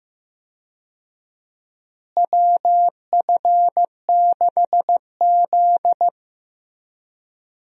Morse code, heard as a keyed tone, about 15 wpm, 700 Hz. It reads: WF6Z
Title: USA Call Signs - 4 Characters - Repeated 3x - 15wpm